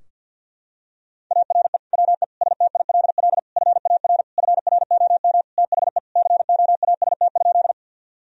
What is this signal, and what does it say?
Morse code, heard as a keyed tone, about 52 wpm, 700 Hz. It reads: ARE WE STILL FAR FROM THE COAST?
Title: Sentences from Top 1000 Words - Review Part 4 - 50wpm